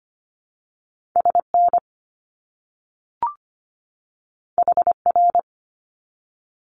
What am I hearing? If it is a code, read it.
Morse code, heard as a keyed tone, about 25 wpm, 700 Hz. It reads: SN HR
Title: Abbreviations 25wpm